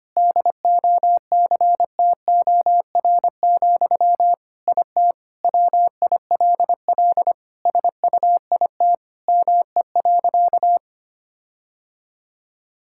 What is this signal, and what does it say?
Morse code, heard as a keyed tone, about 25 wpm, 700 Hz. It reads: DOCTOR, IT WILL SUIT ME.
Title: Sentences from Top 1600 Words 25wpm